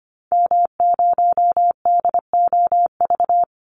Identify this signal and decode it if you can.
Morse code, heard as a keyed tone, about 25 wpm, 700 Hz. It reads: M0DOV
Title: Call Signs Repeated 2x - 25wpm